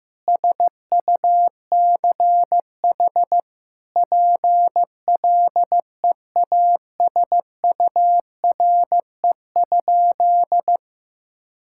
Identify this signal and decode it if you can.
Morse code, heard as a keyed tone, about 15 wpm, 700 Hz. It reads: SUCH PLEASURE?
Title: Sentences from Top 1600 Words 15wpm